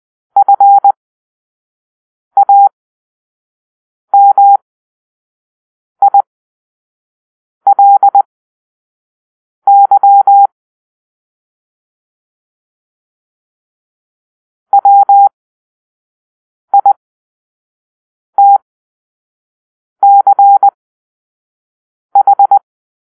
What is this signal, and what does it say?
Morse code, heard as a keyed tone, about 20 wpm, spaced as for 5 wpm, 800 Hz. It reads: FAMILY WITCH